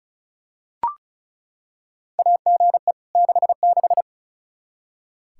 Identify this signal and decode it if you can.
Morse code, heard as a keyed tone, about 36 wpm, 700 Hz. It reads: AGE 66
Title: Practice QSO - Exchange 3 - More Info - 35wpm